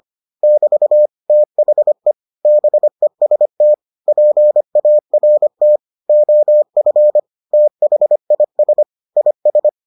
Morse code, heard as a keyed tone, about 25 wpm, 600 Hz. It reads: <BT> THE BEST PART OF THIS IS